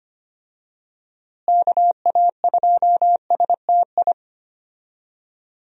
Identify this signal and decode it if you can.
Morse code, heard as a keyed tone, about 25 wpm, 700 Hz. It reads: KA2STI
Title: USA Call Signs - 6 Characters - Repeated 2x - 25wpm